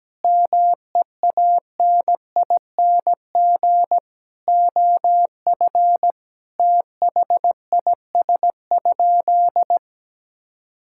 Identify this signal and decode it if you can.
Morse code, heard as a keyed tone, about 17 wpm, 700 Hz. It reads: MEANING OF THIS?
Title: Sentences from Top 1000 Words - Review Part 6 - 17wpm